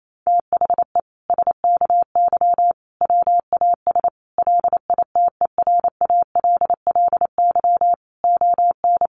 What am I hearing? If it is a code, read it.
Morse code, heard as a keyed tone, about 28 wpm, 700 Hz. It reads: THE SKY WAS LITERALLY ON
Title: Sentences from Top 1400 Words 28wpm